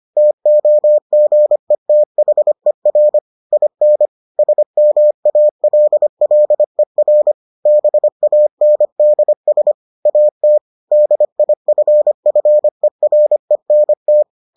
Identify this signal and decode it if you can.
Morse code, heard as a keyed tone, about 25 wpm, 600 Hz. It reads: TOGETHER IN SMALLER BANDS AT DIFFERENT